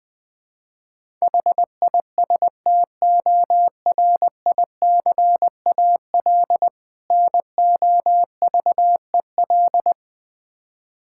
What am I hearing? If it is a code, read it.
Morse code, heard as a keyed tone, about 20 wpm, 700 Hz. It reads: HISTORICAL NOVEL
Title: Difficult Words with Hints - 20wpm